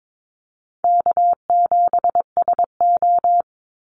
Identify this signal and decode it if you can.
Morse code, heard as a keyed tone, about 22 wpm, 700 Hz. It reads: K7SO
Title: NA SOTA Activator Callsigns 22wpm